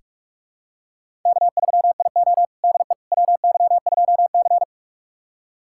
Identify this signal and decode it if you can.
Morse code, heard as a keyed tone, about 46 wpm, 700 Hz. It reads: K3IO DE WY2C
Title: USA Callsign DE Callsign - 4 Characters - 45wpm